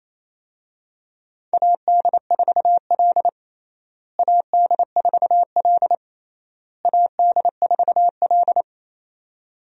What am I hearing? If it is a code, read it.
Morse code, heard as a keyed tone, about 28 wpm, 700 Hz. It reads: AD4L AD4L AD4L
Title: USA Call Signs - 4 Characters - Repeated 3x - 28wpm